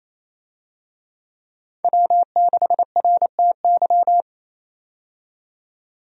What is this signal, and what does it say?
Morse code, heard as a keyed tone, about 30 wpm, 700 Hz. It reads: W6RTY
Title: USA Call Signs - 5 Characters - 28wpm